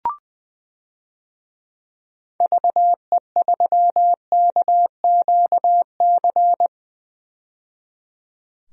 Morse code, heard as a keyed tone, about 20 wpm, 700 Hz. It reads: VE3KQC
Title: DXCC-100 Call Signs - 6 Characters - 20wpm